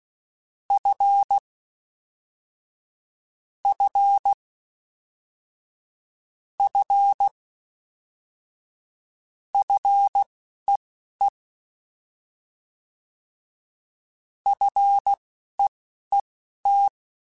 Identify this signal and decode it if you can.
Morse code, heard as a keyed tone, about 16 wpm, spaced as for 11 wpm, 800 Hz. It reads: F F F FEE FEET